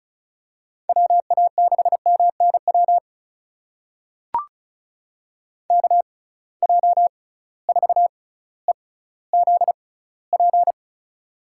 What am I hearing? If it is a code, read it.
Morse code, heard as a keyed tone, about 34 wpm, spaced as for 18 wpm, 700 Hz. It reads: WA6MNW KJ4EZP
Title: USA Call Signs - 6 Characters - 6x Character Spacing - 35wpm